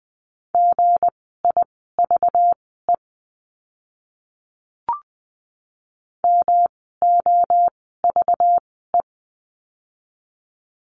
Morse code, heard as a keyed tone, about 20 wpm, spaced as for 15 wpm, 700 Hz. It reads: GIVE MOVE
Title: 237 Lesson 38f - Letter q - Words 20wpm